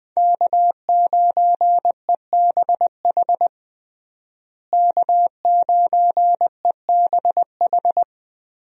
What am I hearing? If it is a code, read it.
Morse code, heard as a keyed tone, about 20 wpm, 700 Hz. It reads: K9EBH K9EBH
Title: USA Call Signs - 5 Characters - Repeated 2x - 20wpm